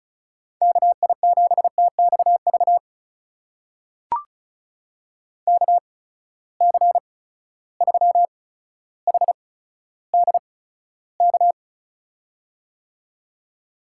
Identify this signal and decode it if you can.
Morse code, heard as a keyed tone, about 36 wpm, spaced as for 15 wpm, 700 Hz. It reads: KI7TXV KC3HDK
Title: USA Call Signs - 6 Characters - 8x Character Spacing - 35wpm